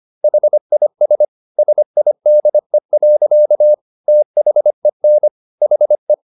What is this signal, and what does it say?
Morse code, heard as a keyed tone, about 25 wpm, 600 Hz. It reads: HIS SIDE. THEN HE